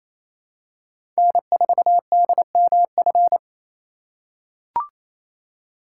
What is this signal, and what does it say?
Morse code, heard as a keyed tone, about 28 wpm, 700 Hz. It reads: N4DMF
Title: USA Call Signs - 5 Characters - 4x Character Spacing - 28wpm